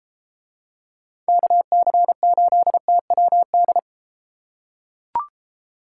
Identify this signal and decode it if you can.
Morse code, heard as a keyed tone, about 32 wpm, 700 Hz. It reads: KC8TWD